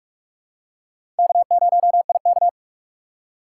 Morse code, heard as a keyed tone, about 46 wpm, 700 Hz. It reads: K0IK
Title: USA Call Signs - 4 Characters - 3x Character Spacing - 45wpm